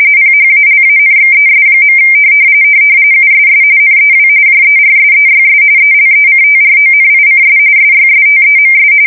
RTTY, about 45.45 baud. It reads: PHALANXES. USS NEW JERSEYS ELECTRONICS SYSTEMS
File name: fsk170 / battleshipnj